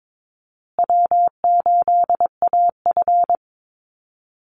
Morse code, heard as a keyed tone, about 22 wpm, 700 Hz. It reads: W8AF
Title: USA Call Signs Repeated 2x 22wpm